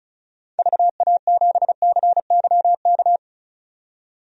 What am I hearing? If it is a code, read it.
Morse code, heard as a keyed tone, about 36 wpm, 700 Hz. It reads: VA7CYK
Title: DXCC-100 Call Signs - 6 Characters - 35wpm